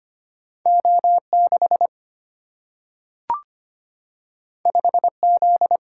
Morse code, heard as a keyed tone, about 25 wpm, 700 Hz. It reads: O 6 5 Z
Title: Single-Letter Single-Number 25wpm